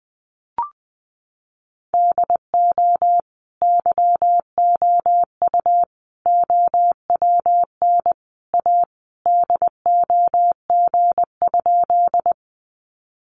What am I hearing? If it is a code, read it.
Morse code, heard as a keyed tone, about 20 wpm, 700 Hz. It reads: DO YOU OWN A DOG?
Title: Sentences from Top 400 Words 20wpm